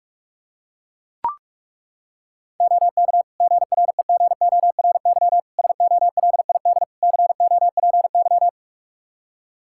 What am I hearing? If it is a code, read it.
Morse code, heard as a keyed tone, about 42 wpm, 700 Hz. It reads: OK GREGORY SOLID COPY
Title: Practice QSO - Exchange 3 - More Info - 45wpm